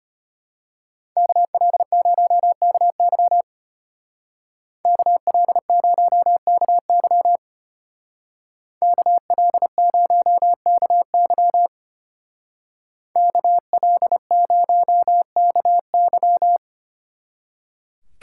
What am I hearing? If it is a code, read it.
Morse code, heard as a keyed tone, about 30 wpm, 700 Hz. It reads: KL0KY KL0KY KL0KY KL0KY
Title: Speed Racing - USA 2x2 Call Signs 25wpm